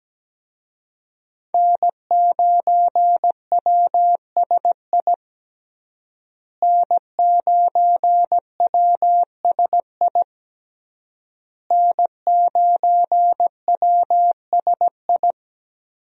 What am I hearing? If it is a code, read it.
Morse code, heard as a keyed tone, about 17 wpm, 700 Hz. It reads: N9WSI N9WSI N9WSI
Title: USA Call Signs - 5 Characters - Repeated 3x - 17wpm